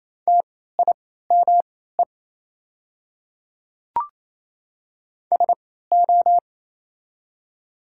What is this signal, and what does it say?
Morse code, heard as a keyed tone, about 28 wpm, spaced as for 16 wpm, 700 Hz. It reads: TIME SO
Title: 3x Character Spacing - Top 100 Words - 28wpm